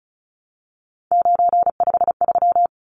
Morse code, heard as a keyed tone, about 34 wpm, 700 Hz. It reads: 953
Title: Numbers - 3-Digit 35wpm